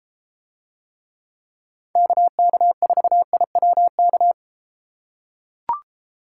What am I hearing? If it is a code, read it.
Morse code, heard as a keyed tone, about 32 wpm, 700 Hz. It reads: KK4IWK